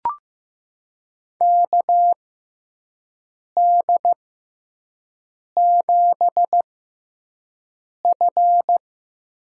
Morse code, heard as a keyed tone, about 15 wpm, spaced as for 5 wpm, 700 Hz. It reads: KD7F
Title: USA Call Signs - 5 Characters - 6x Character Spacing - 15wpm